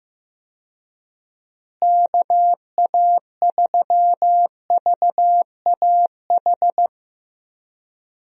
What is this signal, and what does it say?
Morse code, heard as a keyed tone, about 15 wpm, 700 Hz. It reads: KA3VAH
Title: USA Call Signs - 6 Characters - Repeated 2x - 15wpm